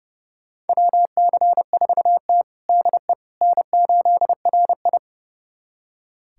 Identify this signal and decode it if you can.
Morse code, heard as a keyed tone, about 32 wpm, 700 Hz. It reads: WC4T DE N8RI
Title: USA Callsign DE Callsign - 4 Characters - 30wpm